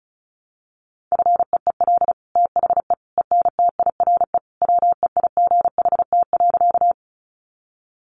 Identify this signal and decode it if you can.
Morse code, heard as a keyed tone, about 34 wpm, 700 Hz. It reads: FEEL THE ENTIRE WEIGHT.
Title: Sentences from Top 800 Words 35wpm